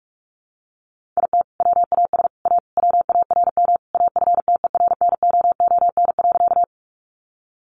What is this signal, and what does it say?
Morse code, heard as a keyed tone, about 44 wpm, 700 Hz. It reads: IT WAS A WARM AFTERNOON.
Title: Sentences from Top 1500 Words 45wpm